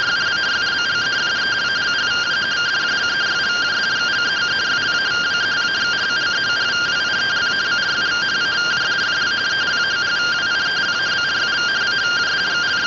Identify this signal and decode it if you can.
Two-tone teleprinter signal, about 170 Hz shift.
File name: findings / CT 1 Scout 0001.gen-38 rtty rx